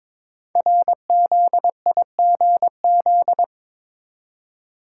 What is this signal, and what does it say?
Morse code, heard as a keyed tone, about 22 wpm, 700 Hz. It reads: RZIGZ
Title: Code Groups - 5 Letters - 22wpm